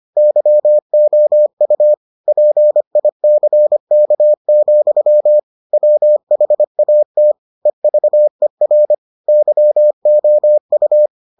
Morse code, heard as a keyed tone, about 25 wpm, 600 Hz. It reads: YOU PICK, WHAT EVER YOU